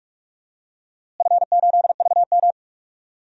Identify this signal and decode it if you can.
Morse code, heard as a keyed tone, about 47 wpm, 700 Hz. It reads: F8VM
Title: DXCC-100 Call Signs - 4 Characters - Repeated 3x - 45wpm